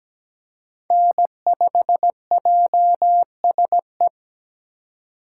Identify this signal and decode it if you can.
Morse code, heard as a keyed tone, about 20 wpm, 700 Hz. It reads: N5JSE
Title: Speed Racing - USA 1x3 Call Signs 17wpm